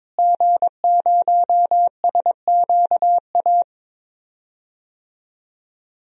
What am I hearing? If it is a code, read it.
Morse code, heard as a keyed tone, about 22 wpm, 700 Hz. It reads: G0SQA
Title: DXCC-100 Call Signs - 5 Characters - Repeated 2x - 22wpm